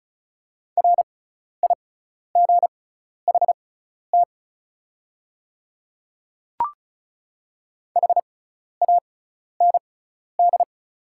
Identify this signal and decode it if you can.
Morse code, heard as a keyed tone, about 36 wpm, spaced as for 12 wpm, 690 Hz. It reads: RIGHT HAND
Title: 6x Character Spacing - Top 300 Words - 35wpm